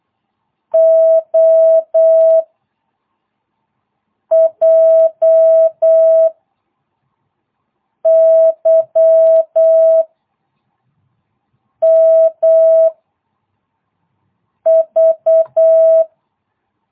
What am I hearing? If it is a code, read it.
Morse code, heard as a keyed tone, about 8 wpm, 660 Hz. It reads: O J Y M V